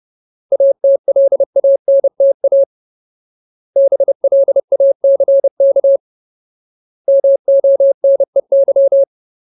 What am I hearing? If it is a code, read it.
Morse code, heard as a keyed tone, about 30 wpm, 550 Hz. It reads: ATLANTA BLACK MONEY
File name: Ultimate Word List / 30WPM Ultimate